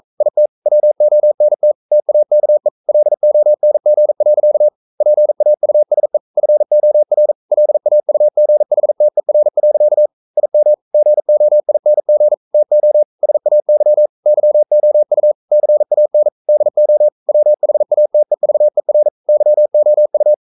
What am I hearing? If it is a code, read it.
Morse code, heard as a keyed tone, about 42 wpm, 600 Hz. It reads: IT WONT TAKE LONG. PAUSE FOR LAUGHTER. IM GOING TO SAY YOU CAN DO WHATEVER YOU
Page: Romanen Never af Ken Follet på engelsk.